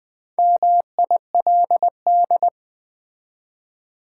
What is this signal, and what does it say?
Morse code, heard as a keyed tone, about 20 wpm, 700 Hz. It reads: MILD
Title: Build A Word - 2 Letters At A Time - 4-Letter Words - 20wpm